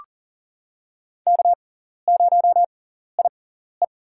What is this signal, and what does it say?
Morse code, heard as a keyed tone, about 39 wpm, spaced as for 14 wpm, 700 Hz. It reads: K0IE
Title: USA Call Signs - 6x Character Spacing - 40wpm